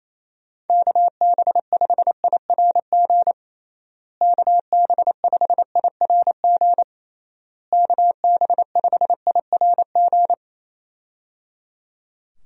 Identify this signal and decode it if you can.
Morse code, heard as a keyed tone, about 27 wpm, 700 Hz. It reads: KB5IRG KB5IRG KB5IRG
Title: USA Call Signs - 6 Characters - Repeated 3x - 28wpm